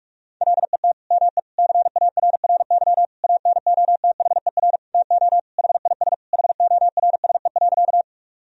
Morse code, heard as a keyed tone, about 41 wpm, 700 Hz. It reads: LET ME CARRY ANOTHER TO HIS HORSE.
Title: Sentences from Top 1000 Words - Review Part 1 - 45wpm